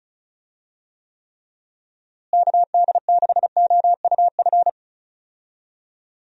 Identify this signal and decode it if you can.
Morse code, heard as a keyed tone, about 36 wpm, 700 Hz. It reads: KD6OUF
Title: USA Call Signs - 6 Characters - Repeated 3x - 35wpm